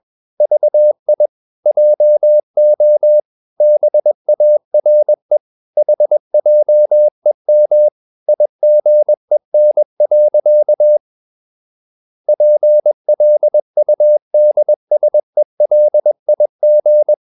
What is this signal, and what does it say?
Morse code, heard as a keyed tone, about 21 wpm, 600 Hz. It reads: VI JO BARE HJEM IGEN. PLUDSELIG
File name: Ferie QRS 21wpm